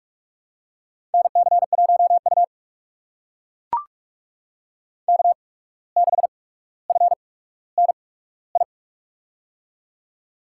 Morse code, heard as a keyed tone, about 42 wpm, spaced as for 17 wpm, 700 Hz. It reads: NC1U K6FNI
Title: USA Call Signs - 8x Character Spacing - 45wpm